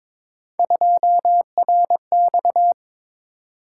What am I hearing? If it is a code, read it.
Morse code, heard as a keyed tone, about 22 wpm, 700 Hz. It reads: 2RX
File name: practice / 22wpm / Three Letter-Number Combinations - 22wpm